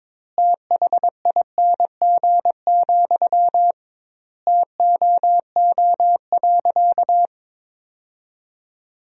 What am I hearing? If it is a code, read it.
Morse code, heard as a keyed tone, about 22 wpm, 700 Hz. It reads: THING, TOO.
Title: 2x Word-Spacing - Sentences from Top 200 Words - 22wpm